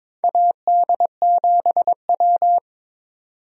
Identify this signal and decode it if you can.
Morse code, heard as a keyed tone, about 22 wpm, 700 Hz. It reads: AD7W
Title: USA Call Signs - 4 Characters - 22wpm